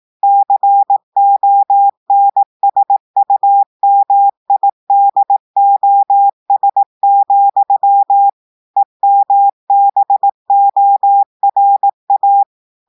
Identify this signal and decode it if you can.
Morse code, heard as a keyed tone, about 18 wpm, 800 Hz. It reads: CONSUMIDOS, EMBORA
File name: portugais texte en clair 2 vitesse 18 mots minutes